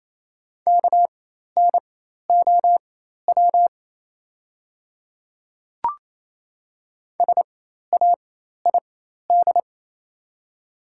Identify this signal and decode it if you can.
Morse code, heard as a keyed tone, about 28 wpm, spaced as for 13 wpm, 700 Hz. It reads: KNOW SAID
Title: 4x Character Spacing - Top 100 Words - 28wpm